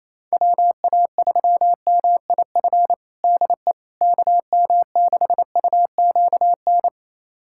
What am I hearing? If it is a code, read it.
Morse code, heard as a keyed tone, about 28 wpm, 700 Hz. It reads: WA3MIF DE KM6UQN
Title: USA Callsign DE Callsign - 6 Characters - 28wpm